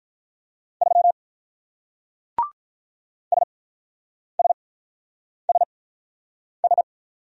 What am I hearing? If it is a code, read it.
Morse code, heard as a keyed tone, about 43 wpm, spaced as for 9 wpm, 700 Hz. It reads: 3 SSSS
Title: Speed-Racing Single Letter-Number 35wpm